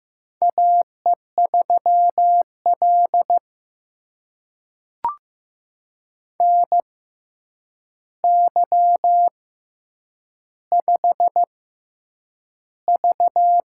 Morse code, heard as a keyed tone, about 15 wpm, spaced as for 8 wpm, 700 Hz. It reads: AE3L NY5V
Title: USA Call Signs - 4 Characters - 6x Character Spacing - 15wpm